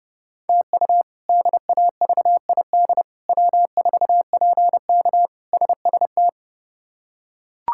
Morse code, heard as a keyed tone, about 30 wpm, 700 Hz. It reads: TU DAVID W4PK SST
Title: K1USN SST QSO Practice - 30wpm